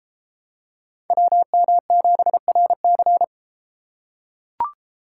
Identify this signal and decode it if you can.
Morse code, heard as a keyed tone, about 31 wpm, 700 Hz. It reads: WM7RC